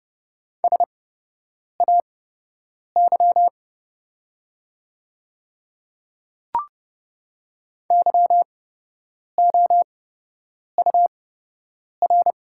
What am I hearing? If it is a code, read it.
Morse code, heard as a keyed tone, about 32 wpm, spaced as for 8 wpm, 700 Hz. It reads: SAY YOUR
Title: 8x Character Spacing - Top 200 Words - 30wpm